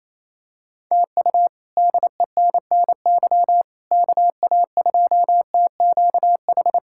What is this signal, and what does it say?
Morse code, heard as a keyed tone, about 27 wpm, 700 Hz. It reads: TU DENNY KA2TQH
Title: K1USN SST QSO Practice - 28wpm